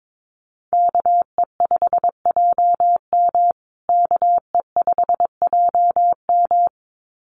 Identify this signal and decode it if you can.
Morse code, heard as a keyed tone, about 22 wpm, 700 Hz. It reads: KE5JM KE5JM
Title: USA Call Signs Repeated 2x 22wpm